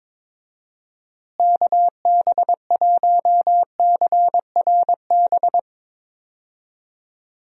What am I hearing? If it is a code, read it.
Morse code, heard as a keyed tone, about 22 wpm, 700 Hz. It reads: KB1CRB
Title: USA Call Signs - 6 Characters - Repeated 2x - 22wpm